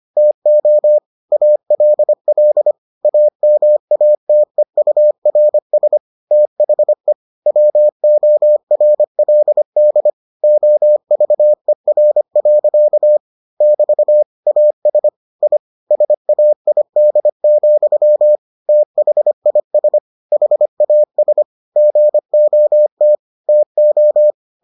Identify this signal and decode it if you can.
Morse code, heard as a keyed tone, about 25 wpm, 600 Hz. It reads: TO ALL AMATEURS THE WORLD OVER. <BT> AS I SAID, THIS HAS GOT TO